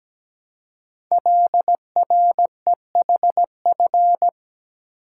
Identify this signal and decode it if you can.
Morse code, heard as a keyed tone, about 17 wpm, 700 Hz. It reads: LREHF
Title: Code Groups - 5 Letters - 17wpm